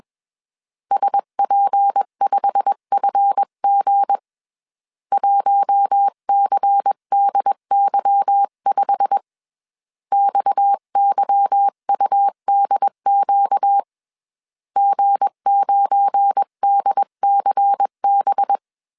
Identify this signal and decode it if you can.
Morse code, heard as a keyed tone, about 21 wpm, 800 Hz. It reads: SP5FG 1CDY5 XYUDQ G9DCB
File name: CW / DET / 1 9 1500C S B 20wpm 2313